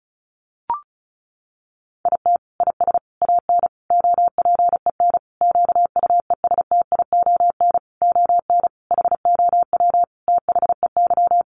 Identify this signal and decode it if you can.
Morse code, heard as a keyed tone, about 34 wpm, 700 Hz. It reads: IT IS AN OPEN QUESTION ON HOW THEY
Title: Sentences from Top 1300 Words 35wpm